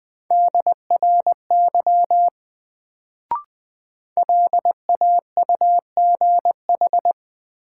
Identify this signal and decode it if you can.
Morse code, heard as a keyed tone, about 20 wpm, 700 Hz. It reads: DRY LAUGH
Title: Sentences from Top 500 Words - New Word Drill - 20wpm